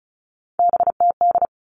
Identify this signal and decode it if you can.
Morse code, heard as a keyed tone, about 34 wpm, 700 Hz. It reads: BTD
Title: All Cut Numbers - 3-Digit 35wpm